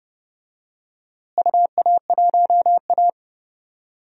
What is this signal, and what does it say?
Morse code, heard as a keyed tone, about 32 wpm, 700 Hz. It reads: UA1A
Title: DXCC-100 Call Signs - 4 Characters - Repeated 2x - 30wpm